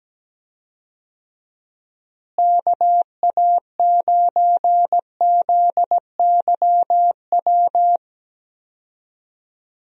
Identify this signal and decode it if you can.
Morse code, heard as a keyed tone, about 17 wpm, 700 Hz. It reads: KA9ZYW
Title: USA Call Signs - 6 Characters - Repeated 3x - 17wpm